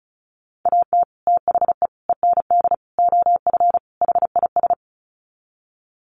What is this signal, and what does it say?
Morse code, heard as a keyed tone, about 34 wpm, 700 Hz. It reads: AT THE END OF HIS
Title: Sets of 5 Words Encore 35wpm